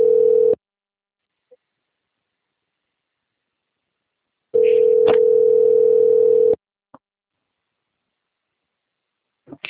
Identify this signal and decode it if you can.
Ringback tone (North America), 440 + 480 Hz.